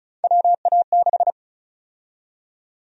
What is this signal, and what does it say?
Morse code, heard as a keyed tone, about 36 wpm, 700 Hz. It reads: WA6
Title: USA Call Sign Prefixes - 35wpm